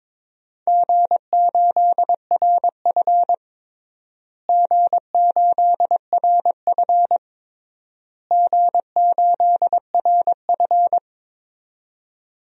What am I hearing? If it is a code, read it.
Morse code, heard as a keyed tone, about 22 wpm, 700 Hz. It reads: G8RF G8RF G8RF
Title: DXCC-100 Call Signs - 4 Characters - Repeated 3x - 22wpm